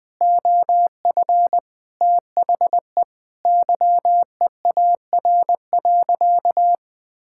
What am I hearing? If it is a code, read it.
Morse code, heard as a keyed tone, about 20 wpm, 700 Hz. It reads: OF THE YEAR.
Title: Sentences with 8-Word Phrases - 20wpm